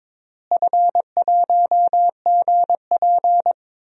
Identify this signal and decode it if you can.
Morse code, heard as a keyed tone, about 22 wpm, 700 Hz. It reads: F1GP
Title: DXCC-100 Call Signs - 4 Characters - Repeated 3x - 22wpm